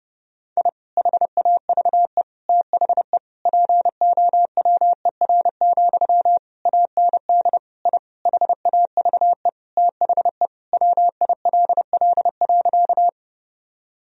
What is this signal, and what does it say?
Morse code, heard as a keyed tone, about 30 wpm, 700 Hz. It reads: I HAVE THE POWER, AND I HAVE THE WILL.
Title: Sentences from Top 1000 Words - Review Part 2 - 30wpm